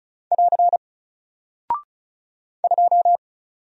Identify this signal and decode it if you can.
Morse code, heard as a keyed tone, about 34 wpm, 700 Hz. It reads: <AR> 2
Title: Single Character (Letter:Number:Punctuation:Prosign) Alt 35wpm